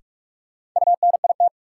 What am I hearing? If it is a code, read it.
Morse code, heard as a keyed tone, about 46 wpm, 700 Hz. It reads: UNIT
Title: Sentences from Top 1000 Words - Review Part 2 - New Word Drill - 45wpm